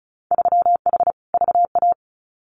Morse code, heard as a keyed tone, about 34 wpm, 700 Hz. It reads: 3H VA
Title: Winter Field Day Exchanges Encore 35wpm